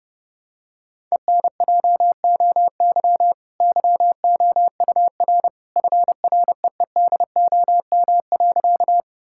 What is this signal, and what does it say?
Morse code, heard as a keyed tone, about 32 wpm, 700 Hz. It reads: ENJOY YOUR FREEDOM.
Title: Sentences from Top 1000 Words - Review Part 5 - 30wpm